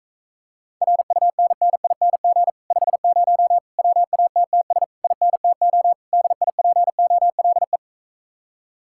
Morse code, heard as a keyed tone, about 45 wpm, 700 Hz. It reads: RUNNING 50 WATTS INTO DIPOLE